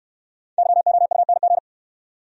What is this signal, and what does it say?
Morse code, heard as a keyed tone, about 63 wpm, 700 Hz. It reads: YOUNG